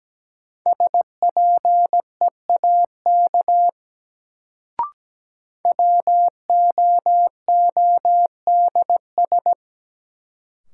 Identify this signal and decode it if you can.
Morse code, heard as a keyed tone, about 17 wpm, 700 Hz. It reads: SPEAK WOODS
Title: Sentences from Top 1000 Words - Review Part 2 - New Word Drill - 17wpm